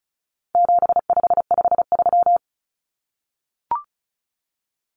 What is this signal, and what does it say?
Morse code, heard as a keyed tone, about 34 wpm, 700 Hz. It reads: 7553
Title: Numbers - 4-Digit 35wpm